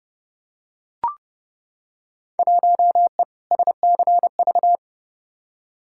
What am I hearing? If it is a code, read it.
Morse code, heard as a keyed tone, about 31 wpm, 700 Hz. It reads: 1E SCV
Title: Field Day Exchanges - 30wpm